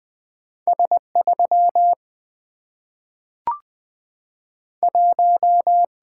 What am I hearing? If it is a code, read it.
Morse code, heard as a keyed tone, about 20 wpm, 700 Hz. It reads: S3 1
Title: Single-Letter Single-Number 20wpm